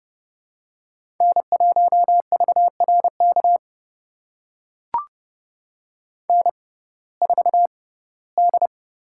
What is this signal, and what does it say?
Morse code, heard as a keyed tone, about 30 wpm, spaced as for 18 wpm, 700 Hz. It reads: N1VRK N4D
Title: USA Call Signs - 5 Characters - 6x Character Spacing - 30wpm